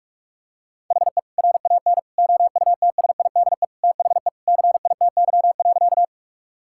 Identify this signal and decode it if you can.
Morse code, heard as a keyed tone, about 44 wpm, 700 Hz. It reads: HE RAN OUTSIDE THE CITY.
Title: Sentences from Top 1000 Words - Review Part 6 - 45wpm